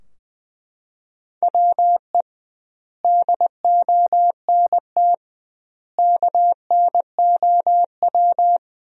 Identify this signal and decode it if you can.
Morse code, heard as a keyed tone, about 20 wpm, 700 Hz. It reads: WE DONT KNOW
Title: 2x Word-Spacing - Sentences from Top 400 Words - 20wpm